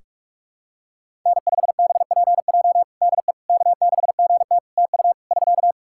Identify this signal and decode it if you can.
Morse code, heard as a keyed tone, about 41 wpm, 700 Hz. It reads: N5BPJ DE K6GT TU <SK>
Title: K1USN SST QSO Practice 45wpm